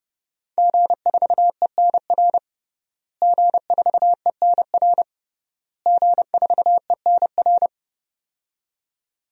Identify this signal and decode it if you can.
Morse code, heard as a keyed tone, about 30 wpm, 700 Hz. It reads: G4ENR G4ENR G4ENR
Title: DXCC-100 Call Signs - 5 Characters - Repeated 3x - 30wpm